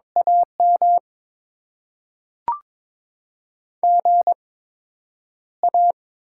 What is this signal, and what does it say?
Morse code, heard as a keyed tone, about 22 wpm, spaced as for 10 wpm, 690 Hz. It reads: AM GA
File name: Top 90 QSO Elements - 8x Character Spacing - 22wpm